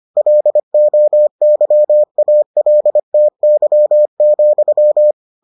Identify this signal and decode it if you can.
Morse code, heard as a keyed tone, about 25 wpm, 600 Hz. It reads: LOYALTY,